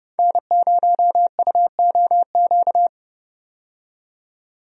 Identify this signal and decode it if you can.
Morse code, heard as a keyed tone, about 31 wpm, 700 Hz. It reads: N0UOQ
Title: USA Call Signs - 5 Characters - 30wpm